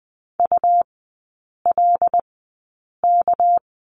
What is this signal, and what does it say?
Morse code, heard as a keyed tone, about 20 wpm, spaced as for 8 wpm, 700 Hz. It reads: ULK
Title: 198 Lesson 32e - Letter k - Three Characters 20wpm